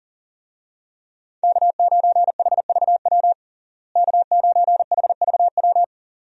Morse code, heard as a keyed tone, about 40 wpm, 700 Hz. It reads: K9HVW K9HVW
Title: USA Call Signs - 5 Characters - Repeated 2x - 40wpm